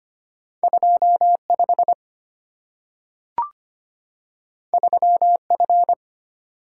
Morse code, heard as a keyed tone, about 25 wpm, 700 Hz. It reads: 25 3F
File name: Two Letter-Number Combinations - 25wpm